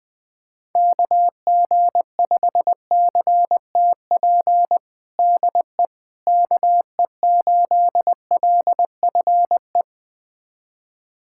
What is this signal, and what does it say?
Morse code, heard as a keyed tone, about 20 wpm, 700 Hz. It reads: KG5CTP DE KE8LFE
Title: USA Callsign DE Callsign - 6 Characters - 20wpm